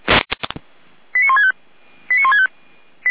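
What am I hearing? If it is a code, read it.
Five-tone selective call (ZVEI), heard as quick runs of five tones, tones of about 70 ms. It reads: 89165 89165
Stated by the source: ZVEI five-tone selective call received on SDR.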